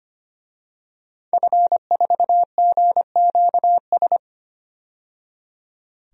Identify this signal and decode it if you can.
Morse code, heard as a keyed tone, about 25 wpm, 700 Hz. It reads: F4GQS
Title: DXCC-100 Call Signs - 5 Characters - 25wpm